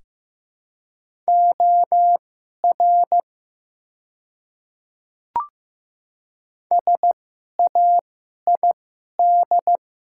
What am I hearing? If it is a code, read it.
Morse code, heard as a keyed tone, about 15 wpm, spaced as for 11 wpm, 700 Hz. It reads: OR SAID
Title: 2x Character Spacing - Top 300 Words - 15wpm